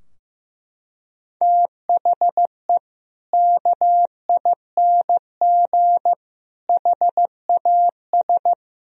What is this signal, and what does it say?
Morse code, heard as a keyed tone, about 15 wpm, 700 Hz. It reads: THE KING HAS
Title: Sentences from Top 1000 Words - Review Part 2 - 15wpm